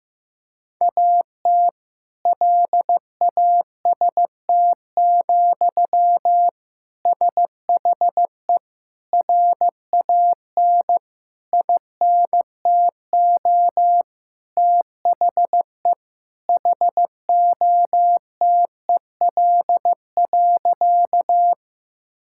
Morse code, heard as a keyed tone, about 15 wpm, 700 Hz. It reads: AT LAST, SHE RAN INTO THE HOTEL.
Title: Sentences from Top 1600 Words 15wpm